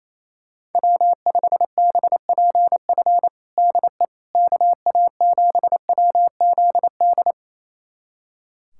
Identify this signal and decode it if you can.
Morse code, heard as a keyed tone, about 28 wpm, 700 Hz. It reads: W5BPF DE KA7WZD
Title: USA Callsign DE Callsign 28wpm